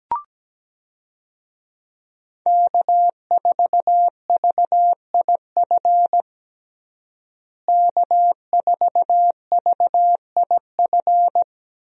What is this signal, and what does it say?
Morse code, heard as a keyed tone, about 17 wpm, 700 Hz. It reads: K4VIF K4VIF
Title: USA Call Signs - 5 Characters - Repeated 2x - 17wpm